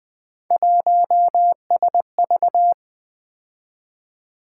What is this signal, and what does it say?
Morse code, heard as a keyed tone, about 20 wpm, 700 Hz. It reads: 1SV
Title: Three Letter-Number Combinations - 20wpm